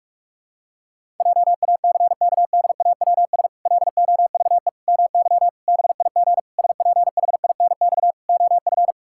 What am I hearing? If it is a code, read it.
Morse code, heard as a keyed tone, about 46 wpm, 700 Hz. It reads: JACKDAWS LOVE MY BIG SPHINX OF
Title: Pangram Challenge v2 - 45wpm